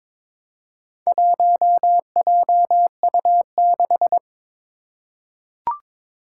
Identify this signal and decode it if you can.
Morse code, heard as a keyed tone, about 22 wpm, 700 Hz. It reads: 1JU6
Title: Four Challenging Letter-Number Combinations 22wpm